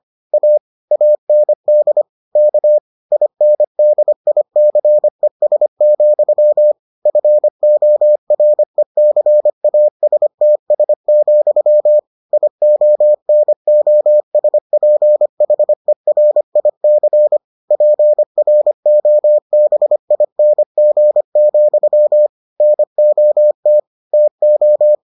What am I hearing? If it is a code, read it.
Morse code, heard as a keyed tone, about 25 wpm, 600 Hz. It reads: A AND K INDICES, FORECASTS, IONOSPHERIC PROBING, NOT TO